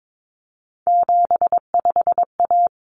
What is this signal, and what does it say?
Morse code, heard as a keyed tone, about 22 wpm, 700 Hz. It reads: 75A
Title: Common Cut Numbers - 3-Digit 22wpm